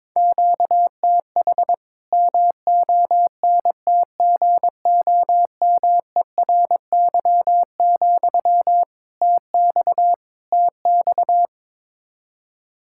Morse code, heard as a keyed tone, about 22 wpm, 700 Hz. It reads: QTH MONTGOMERY, TX TX
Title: K1USN SST QSO Practice 22wpm